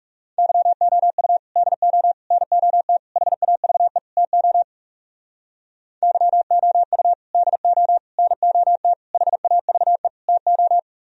Morse code, heard as a keyed tone, about 37 wpm, 700 Hz. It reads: YOU DO NOT HAVE TO YOU DO NOT HAVE TO
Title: Speed-Racing - Sets of 5 Words 30wpm